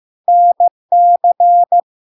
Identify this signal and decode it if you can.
Morse code, heard as a keyed tone, about 15 wpm, 700 Hz. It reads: NC
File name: LARK-FD-2022-Log-in-CW-15wpm-700-Hz